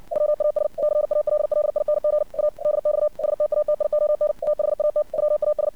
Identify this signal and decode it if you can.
Morse code, heard as a keyed tone, about 56 wpm, 600 Hz. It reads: WAS CALLING A DX STATION THAT WAS
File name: archived / CRSnet